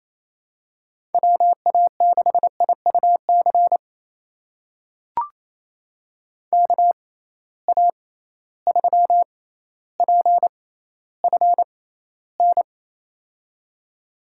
Morse code, heard as a keyed tone, about 28 wpm, spaced as for 14 wpm, 700 Hz. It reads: WA6IUC KA3PFN
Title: USA Call Signs - 6 Characters - 6x Character Spacing - 28wpm